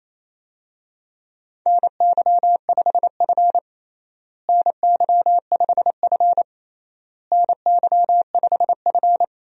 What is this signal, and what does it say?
Morse code, heard as a keyed tone, about 28 wpm, 700 Hz. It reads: NY5F NY5F NY5F
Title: USA Call Signs - 4 Characters - Repeated 3x - 28wpm